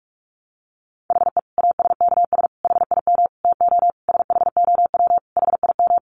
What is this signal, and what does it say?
Morse code, heard as a keyed tone, about 44 wpm, 700 Hz. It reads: HE ASKS HIM TO SHOW HIM
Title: Easy Phrases 45wpm